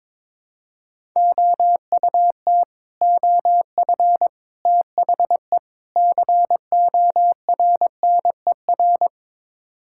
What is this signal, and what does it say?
Morse code, heard as a keyed tone, about 22 wpm, 700 Hz. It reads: OUT OF THE CORNER
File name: practice / 22wpm / Build A Phrase - 5-Word Phrases - 22wpm